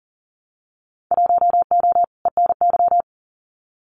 Morse code, heard as a keyed tone, about 40 wpm, 700 Hz. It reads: 1O ENY
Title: Winter Field Day Exchanges 40wpm